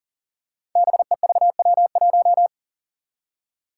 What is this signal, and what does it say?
Morse code, heard as a keyed tone, about 42 wpm, 700 Hz. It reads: BEVW1
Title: Five Challenging Letter-Number Combinations 40wpm